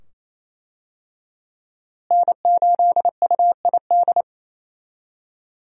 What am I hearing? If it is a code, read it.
Morse code, heard as a keyed tone, about 28 wpm, 700 Hz. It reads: N8UID
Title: USA Call Signs - 5 Characters - Repeated 2x - 28wpm